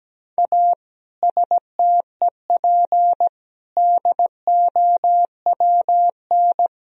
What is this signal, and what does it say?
Morse code, heard as a keyed tone, about 17 wpm, 700 Hz. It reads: A STEP DOWN
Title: Sentences from Top 1000 Words - Review Part 4 - 17wpm